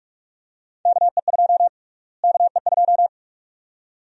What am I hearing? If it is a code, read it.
Morse code, heard as a keyed tone, about 44 wpm, 700 Hz. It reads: KE2 KE2
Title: USA Call Sign Prefixes - Repeated 2x - 45wpm